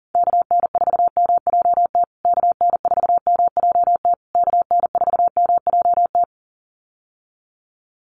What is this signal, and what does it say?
Morse code, heard as a keyed tone, about 40 wpm, 700 Hz. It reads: KN4MJT KN4MJT KN4MJT
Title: USA Call Signs Repeated 3x 40wpm